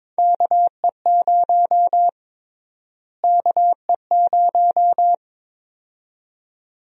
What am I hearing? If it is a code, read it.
Morse code, heard as a keyed tone, about 22 wpm, 700 Hz. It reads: KE0 KE0
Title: USA Call Sign Prefixes - Repeated 2x - 22wpm